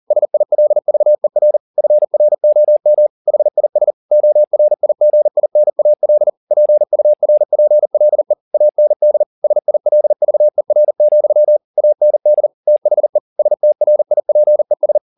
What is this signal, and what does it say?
Morse code, heard as a keyed tone, about 40 wpm, 600 Hz. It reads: SILVER FROM HIS ORIGINAL PURPLE AND SILVER, AND THE STRIPES
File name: the old man and the sea 2 40wpm